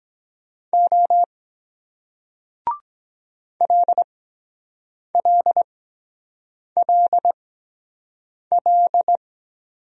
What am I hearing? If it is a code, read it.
Morse code, heard as a keyed tone, about 23 wpm, spaced as for 7 wpm, 700 Hz. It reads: O LLLL